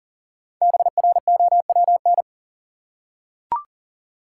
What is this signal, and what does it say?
Morse code, heard as a keyed tone, about 40 wpm, 700 Hz. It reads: BROWN
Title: Five Letter Words - 40wpm